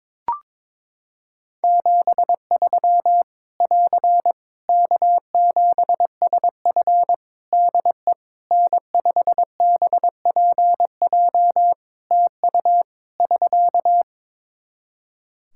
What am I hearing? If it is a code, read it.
Morse code, heard as a keyed tone, about 22 wpm, 700 Hz. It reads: 73 <AR> K7SF DE N5BPJ TU <SK>
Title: Practice QSO - Exchange 4 - Ending - 22wpm